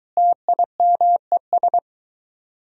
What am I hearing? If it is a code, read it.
Morse code, heard as a keyed tone, about 24 wpm, 700 Hz. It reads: TIMES